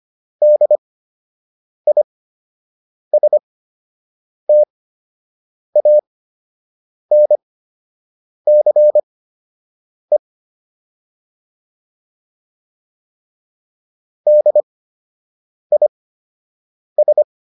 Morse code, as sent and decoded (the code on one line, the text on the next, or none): -.. .. ... - .- -. -.-. / -.. .. ...
DISTANCE DIS